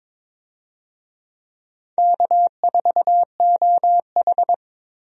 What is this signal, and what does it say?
Morse code, heard as a keyed tone, about 22 wpm, 700 Hz. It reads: K4OH
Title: USA Call Signs - 4 Characters - 22wpm